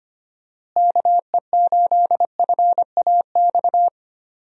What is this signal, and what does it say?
Morse code, heard as a keyed tone, about 25 wpm, 700 Hz. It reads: KE8FAX
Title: USA Call Signs - 6 Characters - Repeated 3x - 25wpm